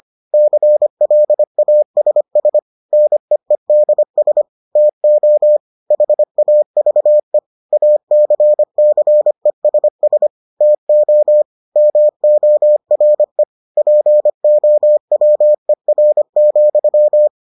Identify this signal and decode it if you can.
Morse code, heard as a keyed tone, about 25 wpm, 600 Hz. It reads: CLASS NEEDS TO HAVE ACCESS TO MORE POWER,